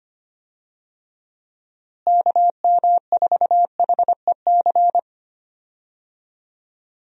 Morse code, heard as a keyed tone, about 25 wpm, 700 Hz. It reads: KM4HEC
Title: USA Call Signs - 6 Characters - Repeated 2x - 25wpm